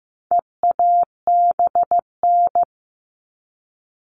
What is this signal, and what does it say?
Morse code, heard as a keyed tone, about 15 wpm, 700 Hz. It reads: EABN
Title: All Cut Numbers - 4-Digit 15wpm